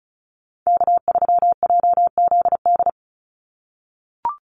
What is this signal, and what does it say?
Morse code, heard as a keyed tone, about 34 wpm, 700 Hz. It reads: K3JZD
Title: NA SOTA Chaser Callsigns 35wpm